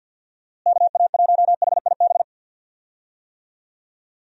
Morse code, heard as a keyed tone, about 46 wpm, 700 Hz. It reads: KA1HIB
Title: USA Call Signs - 6 Characters - Repeated 2x - 50wpm